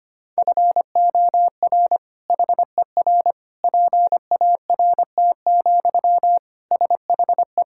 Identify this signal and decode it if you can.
Morse code, heard as a keyed tone, about 25 wpm, 700 Hz. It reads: FOR HER PART, SHE
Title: Sentences from Top 1000 Words - Review Part 5 - 25wpm